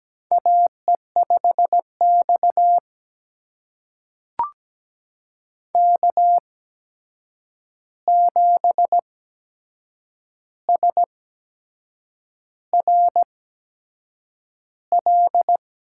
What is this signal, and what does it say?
Morse code, heard as a keyed tone, about 17 wpm, spaced as for 7 wpm, 700 Hz. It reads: AE5X K7SRL
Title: USA Call Signs - 8x Character Spacing - 17wpm